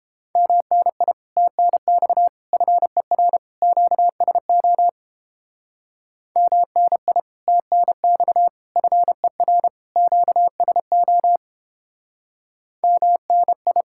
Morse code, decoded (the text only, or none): MNI TNX FER QSO MNI TNX FER QSO MNI